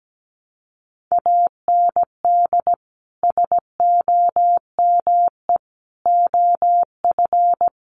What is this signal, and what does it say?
Morse code, heard as a keyed tone, about 17 wpm, 700 Hz. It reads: AND SOME OF
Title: Sets of 4 Words 17wpm